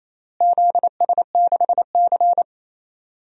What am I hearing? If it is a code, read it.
Morse code, heard as a keyed tone, about 28 wpm, 700 Hz. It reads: ZS6C
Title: DXCC-100 Call Signs - 4 Characters - Repeated 3x - 28wpm